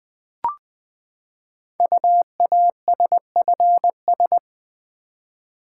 Morse code, heard as a keyed tone, about 20 wpm, 700 Hz. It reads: UASFS
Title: Code Groups - 5 Letters - 20wpm